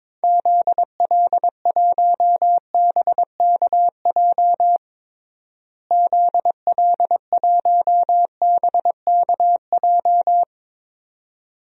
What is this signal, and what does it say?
Morse code, heard as a keyed tone, about 22 wpm, 700 Hz. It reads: ZL1BKJ ZL1BKJ
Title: DXCC-100 Call Signs - 6 Characters - Repeated 2x - 22wpm